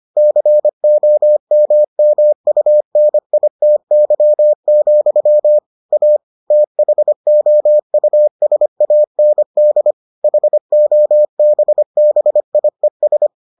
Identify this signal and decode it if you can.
Morse code, heard as a keyed tone, about 25 wpm, 600 Hz. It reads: COMMUNITY, A THOUSAND HOBBIES